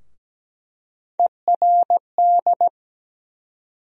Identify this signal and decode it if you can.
Morse code, heard as a keyed tone, about 17 wpm, 700 Hz. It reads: ERD
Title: USA Call Sign Postfixes - 3 Letters 17wpm